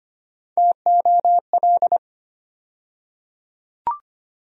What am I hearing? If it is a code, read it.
Morse code, heard as a keyed tone, about 25 wpm, 700 Hz. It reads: TOL
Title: USA Call Sign Postfixes - 3 Letters 25wpm